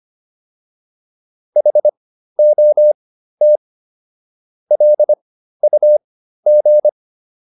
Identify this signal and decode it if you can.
Morse code, heard as a keyed tone, about 25 wpm, spaced as for 13 wpm, 600 Hz. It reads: HOT LUG
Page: intermediate-practice-files